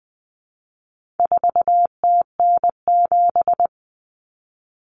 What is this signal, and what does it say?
Morse code, heard as a keyed tone, about 20 wpm, 700 Hz. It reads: 4TN7
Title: Common Cut Numbers - 4-Digit 20wpm